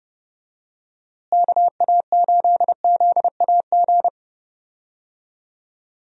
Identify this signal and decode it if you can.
Morse code, heard as a keyed tone, about 30 wpm, 700 Hz. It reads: KA8ZAG
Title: USA Call Signs - 6 Characters - Repeated 2x - 30wpm